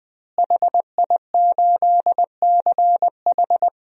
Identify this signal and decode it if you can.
Morse code, heard as a keyed tone, about 21 wpm, 700 Hz. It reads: HI8CH
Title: DXCC-100 Call Signs - 5 Characters - Repeated 3x - 20wpm